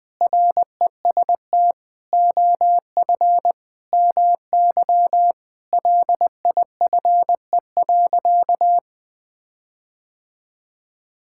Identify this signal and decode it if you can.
Morse code, heard as a keyed tone, about 20 wpm, 700 Hz. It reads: REST OF MY LIFE.
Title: Sentences with 5-Word Phrases - 20wpm